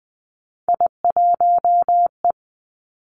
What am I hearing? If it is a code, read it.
Morse code, heard as a keyed tone, about 20 wpm, 700 Hz. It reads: I1E
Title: Three Challenging Letter-Number Combinations 20wpm